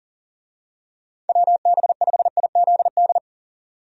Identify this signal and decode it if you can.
Morse code, heard as a keyed tone, about 40 wpm, 700 Hz. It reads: WB5IZD
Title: USA Call Signs - 6 Characters - 40wpm